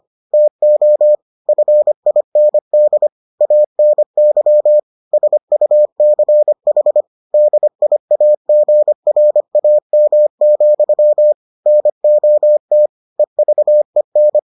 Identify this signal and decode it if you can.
Morse code, heard as a keyed tone, about 25 wpm, 600 Hz. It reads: TO FIND ANY SUCH DIAGRAM, NOT EVEN